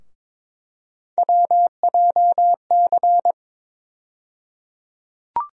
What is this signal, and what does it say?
Morse code, heard as a keyed tone, about 22 wpm, 700 Hz. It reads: WJC
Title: USA Call Sign Postfixes - 3 Letters 22wpm